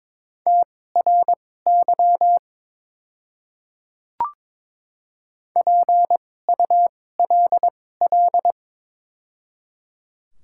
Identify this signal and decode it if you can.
Morse code, heard as a keyed tone, about 22 wpm, spaced as for 16 wpm, 700 Hz. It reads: TRY PULL
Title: 2x Character Spacing - Top 500 Words - 22wpm